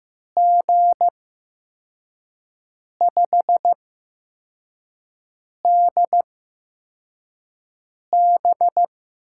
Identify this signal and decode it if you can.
Morse code, heard as a keyed tone, about 15 wpm, spaced as for 4 wpm, 700 Hz. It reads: G5DB
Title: USA Call Signs - 6 Characters - 8x Character Spacing - 15wpm